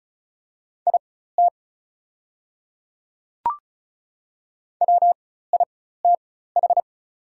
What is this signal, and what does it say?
Morse code, heard as a keyed tone, about 36 wpm, spaced as for 16 wpm, 700 Hz. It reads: IT WITH
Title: 4x Character Spacing - Top 200 Words - 35wpm